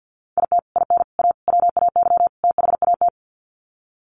Morse code, heard as a keyed tone, about 48 wpm, 700 Hz. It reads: IT IN A WAY THAT
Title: Sets of 5 Words Encore2 50wpm